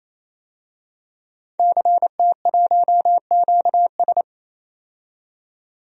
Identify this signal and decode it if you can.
Morse code, heard as a keyed tone, about 28 wpm, 700 Hz. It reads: CT1QS
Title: DXCC-100 Call Signs - 5 Characters - 28wpm